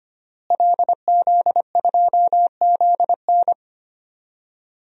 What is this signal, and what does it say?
Morse code, heard as a keyed tone, about 25 wpm, 700 Hz. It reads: LZ2ZN
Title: DXCC-100 Call Signs - 5 Characters - Repeated 3x - 25wpm